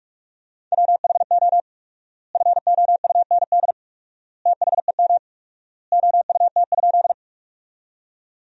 Morse code, heard as a keyed tone, about 47 wpm, 700 Hz. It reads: WHO FOUND THEM OUT?
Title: 4x Word-Spacing - Sentences from Top 300 Words - 45wpm